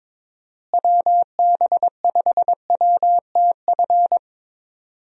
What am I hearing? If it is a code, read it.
Morse code, heard as a keyed tone, about 22 wpm, 700 Hz. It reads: WB5WTF
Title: USA Call Signs - 6 Characters - Repeated 2x - 22wpm